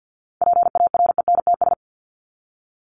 Morse code, heard as a keyed tone, about 50 wpm, 700 Hz. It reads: PARENTS
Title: ES:S Suffix - Top 100 Words with Base Words 50wpm